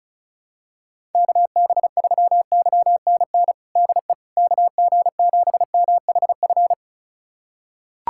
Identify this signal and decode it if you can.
Morse code, heard as a keyed tone, about 36 wpm, 700 Hz. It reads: KB3YNN DE KG7MHF
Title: USA Callsign DE Callsign - 6 Characters - 35wpm